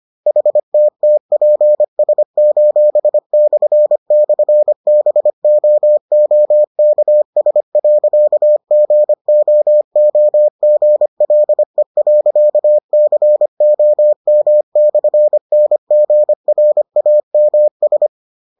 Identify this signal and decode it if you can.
Morse code, heard as a keyed tone, about 25 wpm, 600 Hz. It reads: HTTPS://BOOKS.GOOGLE.COM/NGRAMS